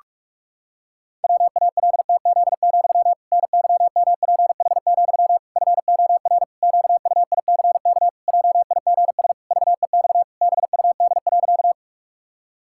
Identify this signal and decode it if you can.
Morse code, heard as a keyed tone, about 46 wpm, 700 Hz. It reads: WALTZ, NYMPH, FOR QUICK JIGS VEX BUD.
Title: Pangram Challenge v2 - 45wpm